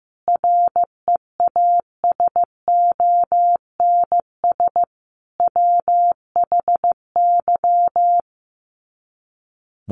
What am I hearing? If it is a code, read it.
Morse code, heard as a keyed tone, about 15 wpm, 700 Hz. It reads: REASONS WHY
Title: Sets of 5 Words 15wpm